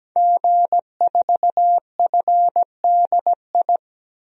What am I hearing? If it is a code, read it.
Morse code, heard as a keyed tone, about 17 wpm, 700 Hz. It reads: G4FDI
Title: DXCC-100 Call Signs - 5 Characters - 17wpm